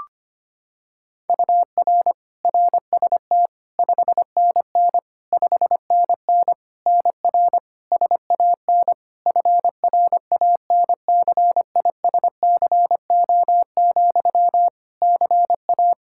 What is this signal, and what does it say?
Morse code, heard as a keyed tone, about 25 wpm, 700 Hz. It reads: UR RST 5NN 5NN NR SAN FRANCISCO, CA
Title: Practice QSO - Exchange 1 - Beginning - 25wpm